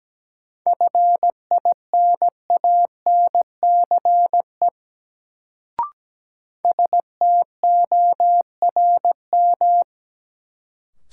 Morse code, heard as a keyed tone, about 17 wpm, 700 Hz. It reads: FINANCE STORM
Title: Sentences from Top 1500 Words - New Word Drill - 17wpm